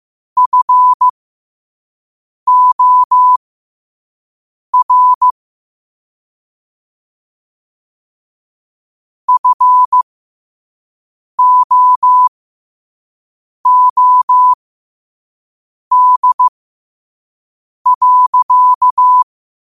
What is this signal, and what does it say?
Morse code, heard as a keyed tone, about 15 wpm, spaced as for 5 wpm, 1000 Hz. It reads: FOR FOOD.
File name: 5wpm